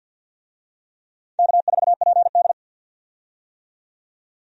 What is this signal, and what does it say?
Morse code, heard as a keyed tone, about 46 wpm, 700 Hz. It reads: K4PD
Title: USA Call Signs - 4 Characters - 50wpm